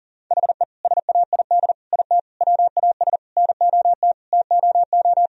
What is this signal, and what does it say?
Morse code, heard as a keyed tone, about 40 wpm, 700 Hz. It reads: HE SAID IT WAS NOT TOO
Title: Sentences from Top 1000 Words - Review Part 2 - 40wpm